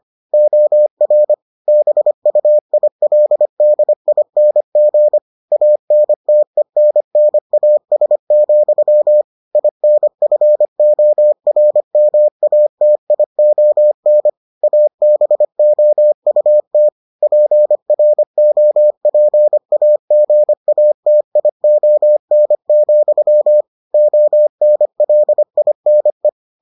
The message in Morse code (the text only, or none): OR BUILDING ANTENNAS, INFORMATION ABOUT PROPAGATION, ONLINE